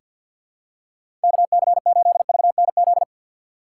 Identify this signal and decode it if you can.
Morse code, heard as a keyed tone, about 49 wpm, 700 Hz. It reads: KC8VNZ
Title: USA Call Signs - 6 Characters - 4x Character Spacing - 50wpm